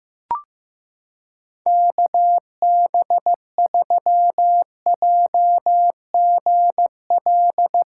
Morse code, heard as a keyed tone, about 15 wpm, 700 Hz. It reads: KB3JGL
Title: USA Callsign DE Callsign 15wpm